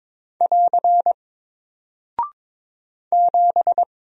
Morse code, heard as a keyed tone, about 22 wpm, 700 Hz. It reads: <AR> 7
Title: Single Character (Letter:Number:Punctuation:Prosign) Alt 22wpm